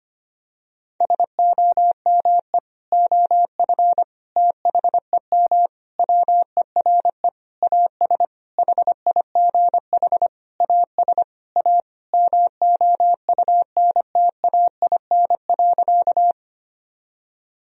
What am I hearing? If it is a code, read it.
Morse code, heard as a keyed tone, about 25 wpm, 700 Hz. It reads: SOME OF THEM WERE AS HIGH AS A MOUNTAIN.
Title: Sentences from Top 1000 Words - Review Part 1 - 25wpm